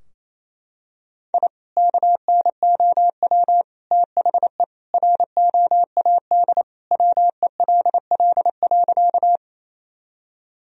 Morse code, heard as a keyed tone, about 28 wpm, 700 Hz. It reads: I KNOW THE ROAD WELL.
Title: Sentences from Top 1000 Words - Review Part 2 - 28wpm